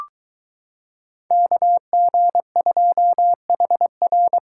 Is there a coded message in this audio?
Yes: Morse code from 1.3 to 4.4 s, heard as a keyed tone, about 22 wpm, 700 Hz. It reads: KG2HR